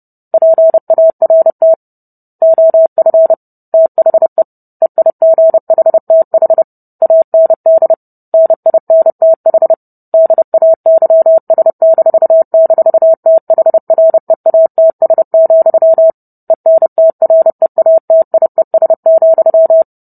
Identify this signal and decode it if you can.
Morse code, heard as a keyed tone, about 30 wpm, 650 Hz. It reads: PART OF THE EIGHTH AND NINTH DAYS--THREATS, ENTREATIES,